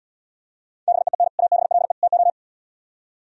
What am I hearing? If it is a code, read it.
Morse code, heard as a keyed tone, about 76 wpm, 700 Hz. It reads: BEEN ABLE TO